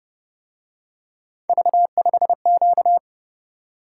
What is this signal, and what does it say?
Morse code, heard as a keyed tone, about 31 wpm, 700 Hz. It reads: V5Q
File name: Speed-Racing - Three Letter-Number Combinations - 25wpm